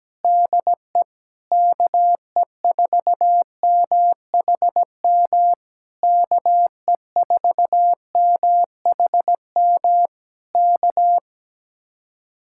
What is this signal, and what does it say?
Morse code, heard as a keyed tone, about 17 wpm, 700 Hz. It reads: DE KE4MHM KE4MHM K
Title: Speed-Racing Common QSO Phrases - 17wpm